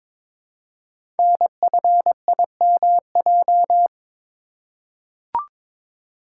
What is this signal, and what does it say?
Morse code, heard as a keyed tone, about 22 wpm, 700 Hz. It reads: NFIMJ
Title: Code Groups - 5 Letters - 22wpm